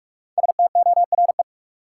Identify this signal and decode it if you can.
Morse code, heard as a keyed tone, about 46 wpm, 700 Hz. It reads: STORE
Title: Sentences from Top 1000 Words - Review Part 5 - New Word Drill - 45wpm